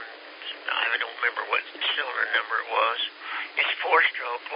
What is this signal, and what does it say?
Emergency Alert System (SAME) tones, heard as harsh buzzing bursts, with speech. It is heard from 0.9 to 3.7 s.